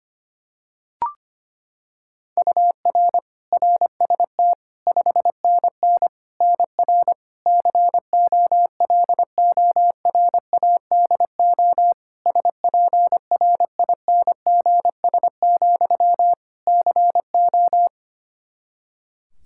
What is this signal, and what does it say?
Morse code, heard as a keyed tone, about 25 wpm, 700 Hz. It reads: UR RST 5NN NR COLORADO SPRINGS, CO
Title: Practice QSO - Exchange 1 - Beginning - 25wpm